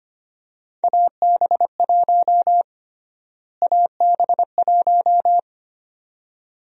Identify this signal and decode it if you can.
Morse code, heard as a keyed tone, about 25 wpm, 700 Hz. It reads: AB1 AB1
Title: USA Call Sign Prefixes - Repeated 2x - 25wpm